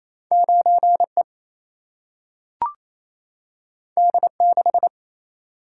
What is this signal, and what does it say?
Morse code, heard as a keyed tone, about 28 wpm, 700 Hz. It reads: 9 E D 6
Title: Single-Letter Single-Number 28wpm